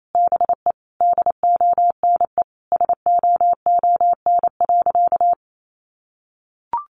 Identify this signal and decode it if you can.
Morse code, heard as a keyed tone, about 28 wpm, 700 Hz. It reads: BE DONE SOON.
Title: Sentences from Top 500 Words 28wpm